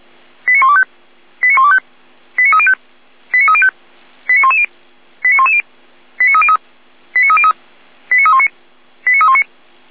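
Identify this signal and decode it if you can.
Five-tone selective call (ZVEI), heard as quick runs of five tones, tones of about 70 ms. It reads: 89125 89125 89305 89305 89110 89110 89303 89303 89319 89319
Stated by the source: ZVEI five-tone selective call received on an SDR.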